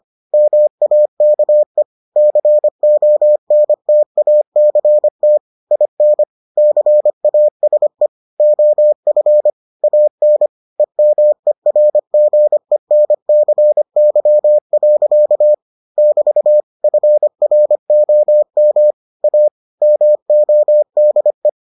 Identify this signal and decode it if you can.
Morse code, heard as a keyed tone, about 25 wpm, 600 Hz. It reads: MAKE CONTACT IN CASE OF AN EMERGENCY. <BT> FROM A MODE